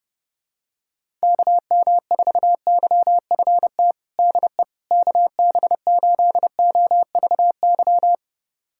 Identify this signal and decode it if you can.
Morse code, heard as a keyed tone, about 30 wpm, 700 Hz. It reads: KM4YFT DE KB8OVY
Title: USA Callsign DE Callsign - 6 Characters - 30wpm